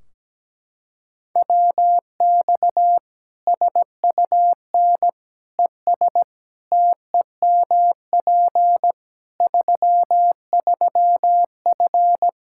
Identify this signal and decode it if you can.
Morse code, heard as a keyed tone, about 17 wpm, 700 Hz. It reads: WX SUN ES TEMP 33F
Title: K1USN SST QSO Practice 17wpm